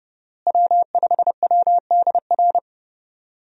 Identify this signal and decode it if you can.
Morse code, heard as a keyed tone, about 32 wpm, 700 Hz. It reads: W5WDR
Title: USA Call Signs - 5 Characters - Repeated 2x - 30wpm